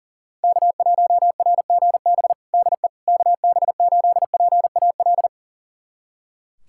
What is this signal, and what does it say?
Morse code, heard as a keyed tone, about 40 wpm, 700 Hz. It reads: K1RGB DE KB8PAL
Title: USA Callsign DE Callsign 40wpm